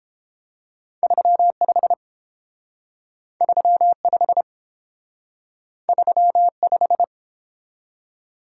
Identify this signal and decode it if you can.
Morse code, heard as a keyed tone, about 32 wpm, 700 Hz. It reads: 35 35 35